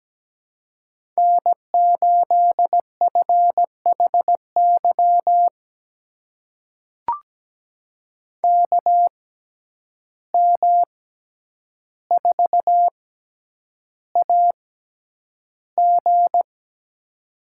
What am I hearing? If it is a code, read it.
Morse code, heard as a keyed tone, about 17 wpm, spaced as for 9 wpm, 700 Hz. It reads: N8FHY KM4AG
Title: USA Call Signs - 5 Characters - 6x Character Spacing - 17wpm